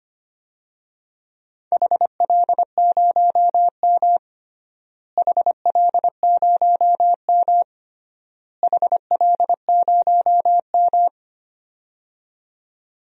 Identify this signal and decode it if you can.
Morse code, heard as a keyed tone, about 25 wpm, 700 Hz. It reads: HL0M HL0M HL0M
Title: DXCC-100 Call Signs - 4 Characters - Repeated 3x - 25wpm